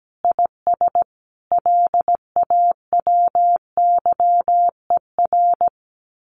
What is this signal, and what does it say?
Morse code, heard as a keyed tone, about 17 wpm, 700 Hz. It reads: IS LAWYER
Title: Common QSO Phrases Encore 17wpm